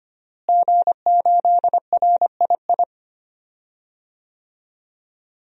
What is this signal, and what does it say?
Morse code, heard as a keyed tone, about 25 wpm, 700 Hz. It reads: G8RII
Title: DXCC-100 Call Signs - 5 Characters - Repeated 3x - 25wpm